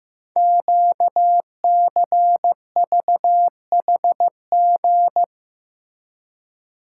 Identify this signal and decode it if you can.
Morse code, heard as a keyed tone, about 15 wpm, 700 Hz. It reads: QCVHG
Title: Code Groups - 5 Letters - 15wpm